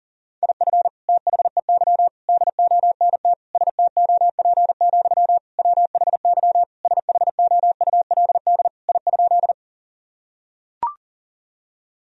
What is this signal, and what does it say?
Morse code, heard as a keyed tone, about 40 wpm, 700 Hz. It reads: IF THEY DONT STOP, WHY SHOULD I?
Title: Sentences from Top 1000 Words - Review Part 1 - 40wpm